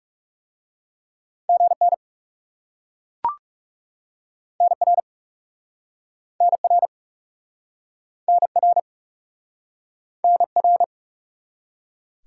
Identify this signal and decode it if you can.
Morse code, heard as a keyed tone, about 42 wpm, 700 Hz. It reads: GN NR NR NR NR